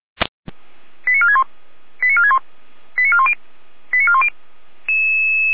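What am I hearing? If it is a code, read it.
Five-tone selective call (ZVEI), heard as quick runs of five tones, tones of about 70 ms. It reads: 89461 89461 89410 89410
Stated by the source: ZVEI five-tone call 89150 received on an SDR.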